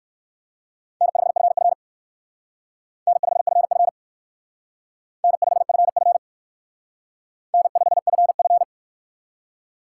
Morse code, heard as a keyed tone, about 58 wpm, 700 Hz. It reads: N5FF N5FF N5FF N5FF